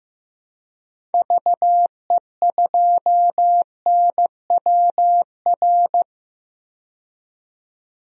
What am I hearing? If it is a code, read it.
Morse code, heard as a keyed tone, about 15 wpm, 700 Hz. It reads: VE2NWR
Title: DXCC-100 Call Signs - 6 Characters - 15wpm